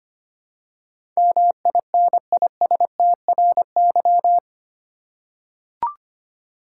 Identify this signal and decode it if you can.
Morse code, heard as a keyed tone, about 25 wpm, 700 Hz. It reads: MINISTRY
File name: practice / 25wpm / 8 Letter Words - First 3 Letters Given - 25wpm